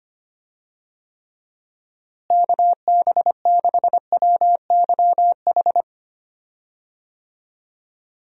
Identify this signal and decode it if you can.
Morse code, heard as a keyed tone, about 25 wpm, 700 Hz. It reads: KB6WYH
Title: USA Call Signs - 6 Characters - Repeated 3x - 25wpm